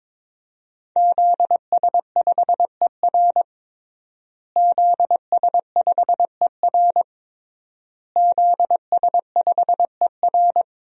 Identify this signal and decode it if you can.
Morse code, heard as a keyed tone, about 22 wpm, 700 Hz. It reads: ZS5ER ZS5ER ZS5ER
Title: DXCC-100 Call Signs - 5 Characters - Repeated 3x - 22wpm